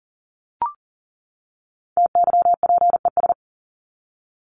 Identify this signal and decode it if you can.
Morse code, heard as a keyed tone, about 40 wpm, 700 Hz. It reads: TYPES
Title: ES:S Suffix - Top 100 Words with Base Words 40wpm